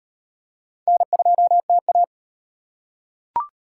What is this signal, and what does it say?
Morse code, heard as a keyed tone, about 38 wpm, 700 Hz. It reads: N2TA